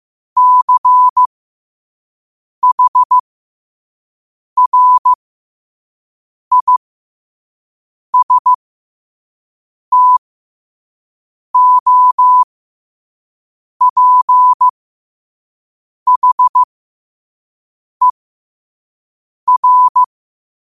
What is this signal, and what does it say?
Morse code, heard as a keyed tone, about 15 wpm, spaced as for 5 wpm, 1000 Hz. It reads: CHRISTOPHER
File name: qotdfiles / 5wpm